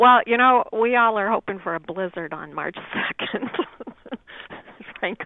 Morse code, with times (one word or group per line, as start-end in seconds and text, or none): none